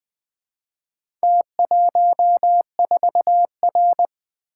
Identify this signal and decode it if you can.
Morse code, heard as a keyed tone, about 25 wpm, 700 Hz. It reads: T14R
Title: Speed-Racing - Four Letter-Number Combinations - 20wpm